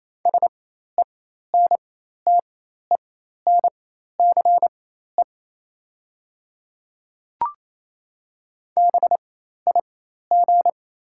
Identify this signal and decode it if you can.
Morse code, heard as a keyed tone, about 28 wpm, spaced as for 13 wpm, 690 Hz. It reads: SENTENCE BIG
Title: 4x Character Spacing - Top 200 Words - 28wpm